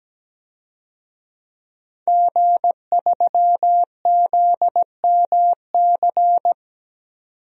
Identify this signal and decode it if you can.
Morse code, heard as a keyed tone, about 17 wpm, 700 Hz. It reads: G3ZMC
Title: DXCC-100 Call Signs - 5 Characters - 17wpm